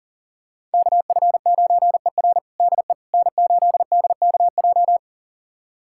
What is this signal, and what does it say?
Morse code, heard as a keyed tone, about 40 wpm, 700 Hz. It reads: KF9ER DE N8DKJ
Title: USA Callsign DE Callsign 40wpm